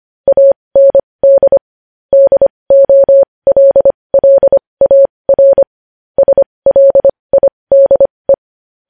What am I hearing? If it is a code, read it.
Morse code, heard as a keyed tone, about 25 wpm, spaced as for 20 wpm, 550 Hz. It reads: AND DOLLAR SLIDE
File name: podcast / News headlines-1x-25x20